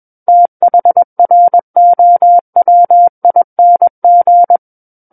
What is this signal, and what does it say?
Morse code, heard as a keyed tone, about 21 wpm, 700 Hz. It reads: THROWING
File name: Book One / 12-Chapter 12 - 21 WPM